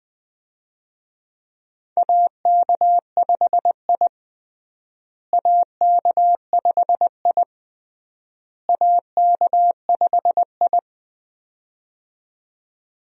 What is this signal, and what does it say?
Morse code, heard as a keyed tone, about 20 wpm, 700 Hz. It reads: AK5I AK5I AK5I
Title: USA Call Signs - 4 Characters - Repeated 3x - 20wpm